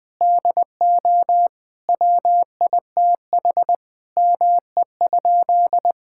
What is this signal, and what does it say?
Morse code, heard as a keyed tone, about 21 wpm, 700 Hz. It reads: DO WITH ME?
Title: Sentences with 7-Word Phrases - Spoken Beginning - 20wpm